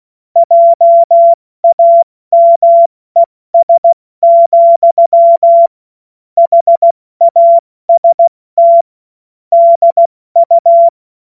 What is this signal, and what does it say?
Morse code, heard as a keyed tone, about 16 wpm, 680 Hz. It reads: JAMES, HAST DU
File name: Witz-05-August-2025-70-F80